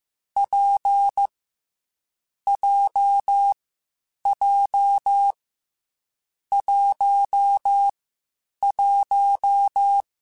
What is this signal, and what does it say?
Morse code, heard as a keyed tone, about 15 wpm, 800 Hz. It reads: P J J 1 1
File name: Difficult Combos / Difficult Combos-mp3s / WPJ WPJ1 @13